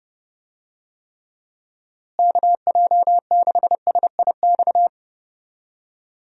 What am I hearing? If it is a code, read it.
Morse code, heard as a keyed tone, about 30 wpm, 700 Hz. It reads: KJ6SIX
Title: USA Call Signs - 6 Characters - Repeated 2x - 30wpm